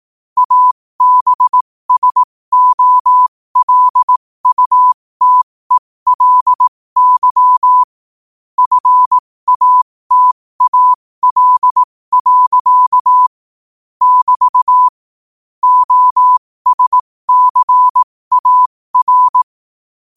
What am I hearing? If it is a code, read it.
Morse code, heard as a keyed tone, about 18 wpm, spaced as for 15 wpm, 1000 Hz. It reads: ABSOLUTELY FATAL. <BT> OSCAR
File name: qotdfiles / 15wpm